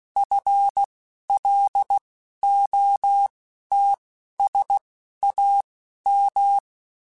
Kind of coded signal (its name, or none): Morse code